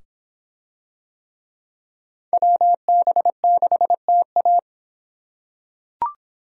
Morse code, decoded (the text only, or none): WB6TA